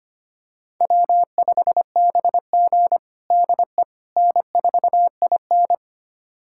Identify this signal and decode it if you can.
Morse code, heard as a keyed tone, about 25 wpm, 700 Hz. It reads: W5BG DE N4IN
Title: USA Callsign DE Callsign - 4 Characters - 25wpm